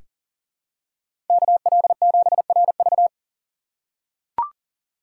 Morse code, heard as a keyed tone, about 37 wpm, 700 Hz. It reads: KL7RV
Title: USA Call Signs - 5 Characters - 3x Character Spacing - 40wpm